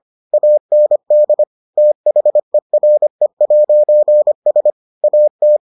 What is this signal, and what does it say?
Morse code, heard as a keyed tone, about 25 wpm, 600 Hz. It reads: AND THERE'S AT